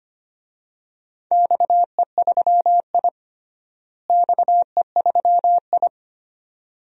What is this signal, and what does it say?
Morse code, heard as a keyed tone, about 25 wpm, 700 Hz. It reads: XE3I XE3I
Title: DXCC-100 Call Signs - 4 Characters - Repeated 2x - 25wpm